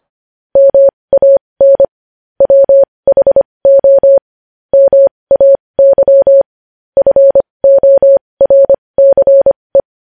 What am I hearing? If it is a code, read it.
Morse code, heard as a keyed tone, about 25 wpm, spaced as for 20 wpm, 550 Hz. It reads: MAN WHO MAY FORCE
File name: podcast / News headlines-1x-25x20